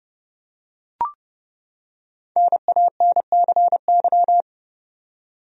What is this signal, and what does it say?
Morse code, heard as a keyed tone, about 30 wpm, 700 Hz. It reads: NANCY
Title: USA Common Names - 30wpm